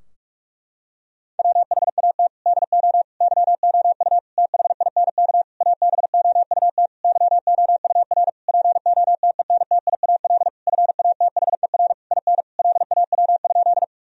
Morse code, heard as a keyed tone, about 41 wpm, 700 Hz. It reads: WHAT DO YOU THINK ABOUT YOUR POTENTIAL FATHER IN LAW?
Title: Sentences from Top 1000 Words - Review Part 4 - 45wpm